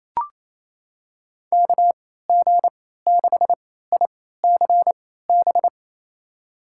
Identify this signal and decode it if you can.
Morse code, heard as a keyed tone, about 28 wpm, spaced as for 16 wpm, 700 Hz. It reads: KG6ICB
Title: USA Call Signs - 3x Character Spacing - 28wpm